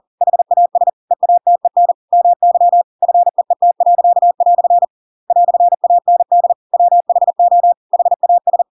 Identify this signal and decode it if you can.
Morse code, heard as a keyed tone, about 40 wpm, 700 Hz. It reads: HAS EATEN MY FEET." "AND WHO HAS
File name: Pinocchio / Pinocchio ch7 40 40 ik1hge